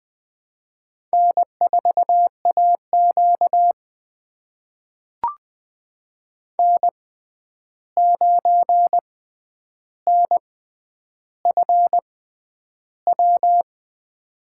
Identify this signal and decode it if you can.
Morse code, heard as a keyed tone, about 20 wpm, spaced as for 10 wpm, 700 Hz. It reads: N4AQ N9NFW
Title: USA Call Signs - 6x Character Spacing - 20wpm